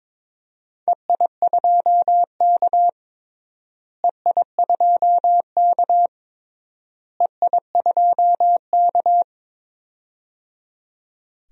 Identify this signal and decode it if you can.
Morse code, heard as a keyed tone, about 21 wpm, 700 Hz. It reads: EI2K EI2K EI2K
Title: DXCC-100 Call Signs - 4 Characters - Repeated 3x - 22wpm